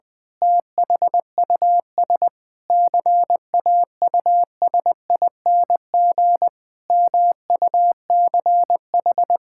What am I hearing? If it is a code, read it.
Morse code, heard as a keyed tone, about 20 wpm, 700 Hz. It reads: THUS CAUSING MUCH
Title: Pangram Challenge v2 - 20wpm